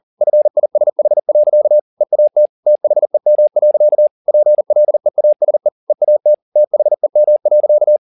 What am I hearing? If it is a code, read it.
Morse code, heard as a keyed tone, about 40 wpm, 600 Hz. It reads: FISH. EAT THEM. PLEASE EAT THEM.